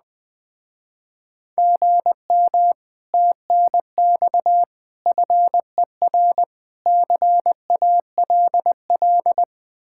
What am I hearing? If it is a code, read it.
Morse code, heard as a keyed tone, about 20 wpm, 700 Hz. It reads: GM TNX FER CALL
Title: Speed-Racing Common QSO Phrases - 17wpm